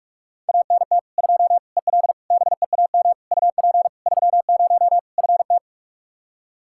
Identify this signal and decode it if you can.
Morse code, heard as a keyed tone, about 41 wpm, 700 Hz. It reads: ANT 2 EL BEAM UP 30 FT
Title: Practice QSO - Exchange 2 - Info - 45wpm